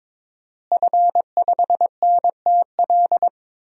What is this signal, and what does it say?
Morse code, heard as a keyed tone, about 22 wpm, 700 Hz. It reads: F5NTL
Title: DXCC-100 Call Signs - 5 Characters - Repeated 2x - 22wpm